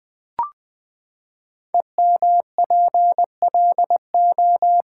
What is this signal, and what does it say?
Morse code, heard as a keyed tone, about 20 wpm, 700 Hz. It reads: EMPLO
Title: Build A Word - 5 Letters At A Time - 10-Letter Words - 20wpm